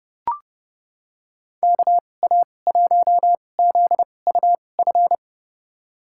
Morse code, heard as a keyed tone, about 32 wpm, spaced as for 22 wpm, 700 Hz. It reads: KA1ZUF
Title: USA Call Signs - 6 Characters - 2x Character Spacing - 30wpm